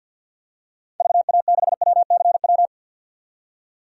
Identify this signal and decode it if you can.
Morse code, heard as a keyed tone, about 48 wpm, 700 Hz. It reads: VA6WCW
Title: DXCC-100 Call Signs - 6 Characters - 50wpm